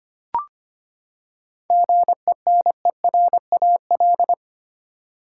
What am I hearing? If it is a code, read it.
Morse code, heard as a keyed tone, about 25 wpm, 700 Hz. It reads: GENERAL
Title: Seven Letter Words - 25wpm